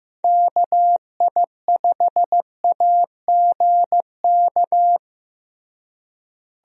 Morse code, heard as a keyed tone, about 15 wpm, 700 Hz. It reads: KI5AGK
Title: USA Call Signs - 6 Characters - Repeated 2x - 15wpm